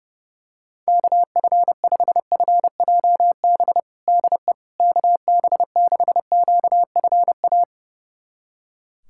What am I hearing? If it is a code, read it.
Morse code, heard as a keyed tone, about 29 wpm, 700 Hz. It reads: KF5FJB DE KB6QFA
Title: USA Callsign DE Callsign - 6 Characters - 30wpm